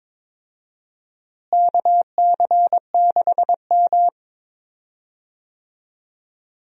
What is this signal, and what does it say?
Morse code, heard as a keyed tone, about 22 wpm, 700 Hz. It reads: KC6M
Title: USA Call Signs - 4 Characters - Repeated 3x - 22wpm